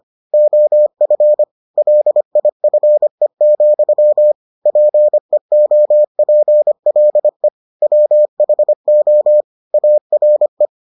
Morse code, heard as a keyed tone, about 25 wpm, 600 Hz. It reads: OF LIFE, PEOPLE WHO ARE